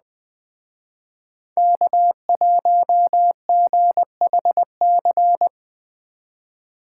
Morse code, heard as a keyed tone, about 20 wpm, 700 Hz. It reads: K1GHC